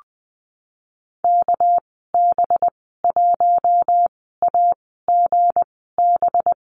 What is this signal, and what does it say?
Morse code, heard as a keyed tone, about 19 wpm, spaced as for 14 wpm, 700 Hz. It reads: KB1AGB
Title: 213 Lesson 34g - Number 8 - Call Signs 20wpm